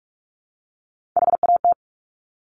Morse code, heard as a keyed tone, about 46 wpm, 700 Hz. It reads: HAT
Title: Top Three Letter Combinations in Words 45wpm